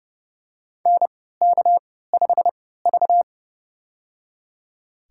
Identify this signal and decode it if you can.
Morse code, heard as a keyed tone, about 30 wpm, spaced as for 17 wpm, 700 Hz. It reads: NK5V
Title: USA Call Signs - 3x Character Spacing - 30wpm